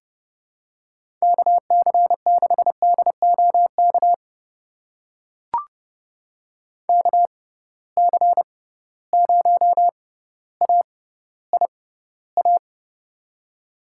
Morse code, heard as a keyed tone, about 30 wpm, spaced as for 15 wpm, 700 Hz. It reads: KC6DOK KC0AIA
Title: USA Call Signs - 6 Characters - 6x Character Spacing - 30wpm